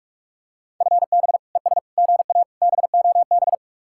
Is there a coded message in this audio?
Yes: Morse code from 0.8 to 3.5 s, heard as a keyed tone, about 45 wpm, 700 Hz. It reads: FB ES GA BOB